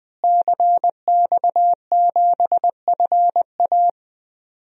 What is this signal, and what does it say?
Morse code, heard as a keyed tone, about 21 wpm, 700 Hz. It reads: CX7FA
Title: DXCC-100 Call Signs - 5 Characters - Repeated 3x - 20wpm